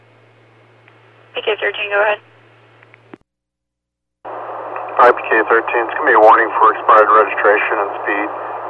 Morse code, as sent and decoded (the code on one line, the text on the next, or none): none